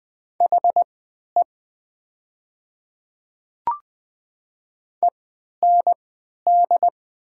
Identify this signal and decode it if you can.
Morse code, heard as a keyed tone, about 20 wpm, spaced as for 11 wpm, 700 Hz. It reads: HE END
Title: 3x Character Spacing - Top 500 Words - 20wpm